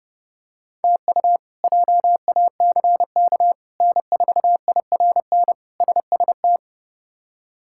Morse code, heard as a keyed tone, about 30 wpm, 700 Hz. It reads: TU JACK N4IRN SST
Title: K1USN SST QSO Practice - 30wpm